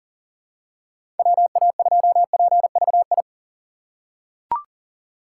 Morse code, heard as a keyed tone, about 39 wpm, 700 Hz. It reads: WA2PVI
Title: USA Call Signs - 6x Character Spacing - 40wpm